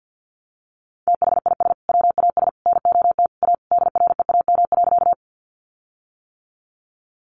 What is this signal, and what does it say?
Morse code, heard as a keyed tone, about 50 wpm, 700 Hz. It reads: THIS WAS NOT A DREAM.
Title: Sentences from Top 1500 Words 50wpm